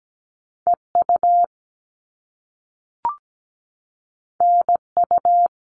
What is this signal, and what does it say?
Morse code, heard as a keyed tone, about 17 wpm, 700 Hz. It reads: EU NU
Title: All Cut Numbers - 2-Digit 17wpm